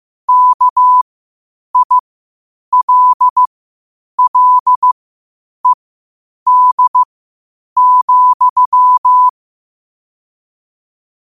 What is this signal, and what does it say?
Morse code, heard as a keyed tone, about 15 wpm, spaced as for 8 wpm, 1000 Hz. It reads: KILLED,
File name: qotdfiles / 8wpm